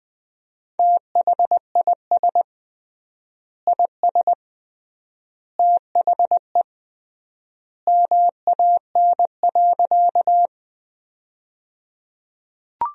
Morse code, heard as a keyed tone, about 20 wpm, 700 Hz. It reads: THIS IS THE MAN.
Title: 3x Word-Spacing - Sentences from Top 200 Words - 20wpm